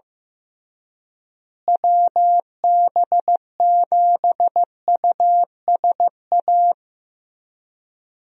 Morse code, heard as a keyed tone, about 15 wpm, 700 Hz. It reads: WB7USA
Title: USA Call Signs - 6 Characters - Repeated 3x - 15wpm